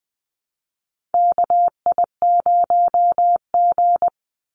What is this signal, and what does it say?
Morse code, heard as a keyed tone, about 20 wpm, 700 Hz. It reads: KI0G
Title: NA SOTA Chaser Callsigns 20wpm